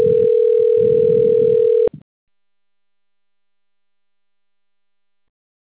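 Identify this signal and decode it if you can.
Ringback tone (North America), 440 + 480 Hz.
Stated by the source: Outbound telephone call recorded on the Sao Roque system.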